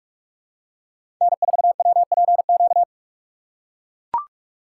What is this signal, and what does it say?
Morse code, heard as a keyed tone, about 43 wpm, 700 Hz. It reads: N4WPQ
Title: USA Call Signs - 4x Character Spacing - 45wpm